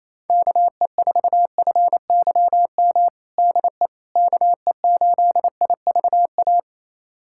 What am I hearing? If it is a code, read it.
Morse code, heard as a keyed tone, about 28 wpm, 700 Hz. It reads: KE4FYM DE KE8IVA
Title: USA Callsign DE Callsign - 6 Characters - 28wpm